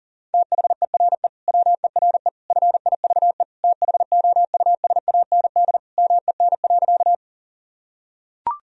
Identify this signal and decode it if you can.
Morse code, heard as a keyed tone, about 37 wpm, 700 Hz. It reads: THERE WERE FIVE THOUSAND MEN.
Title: Sentences from Top 1000 Words - Review Part 3 - 40wpm